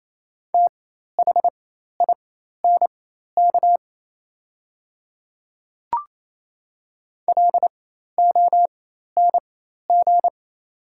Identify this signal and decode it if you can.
Morse code, heard as a keyed tone, about 28 wpm, spaced as for 13 wpm, 700 Hz. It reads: THINK LONG
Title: 4x Character Spacing - Top 500 Words - 28wpm